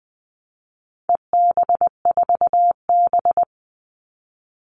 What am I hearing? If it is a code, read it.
Morse code, heard as a keyed tone, about 20 wpm, 700 Hz. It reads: EB4B
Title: All Cut Numbers - 4-Digit 20wpm